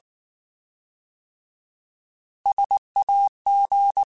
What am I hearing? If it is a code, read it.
Morse code, heard as a keyed tone, about 19 wpm, 800 Hz. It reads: SAG
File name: Alphabetical / Alphabetical mp3s / Ss @18